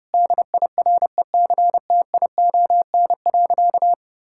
Morse code, heard as a keyed tone, about 32 wpm, 700 Hz. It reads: DIRECTION.
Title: Sentences from Top 1000 Words - Review Part 5 - 30wpm